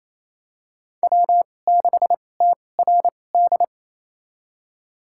Morse code, heard as a keyed tone, about 28 wpm, spaced as for 20 wpm, 700 Hz. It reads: W6TRD
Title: USA Call Signs - 5 Characters - 2x Character Spacing - 28wpm